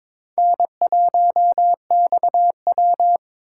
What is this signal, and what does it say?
Morse code, heard as a keyed tone, about 22 wpm, 700 Hz. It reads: N1XW
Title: USA Call Signs - 4 Characters - 2x Character Spacing - 22wpm